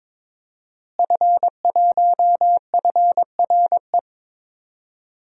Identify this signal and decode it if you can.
Morse code, heard as a keyed tone, about 21 wpm, 700 Hz. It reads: F1FRE
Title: DXCC-100 Call Signs - 5 Characters - Repeated 3x - 22wpm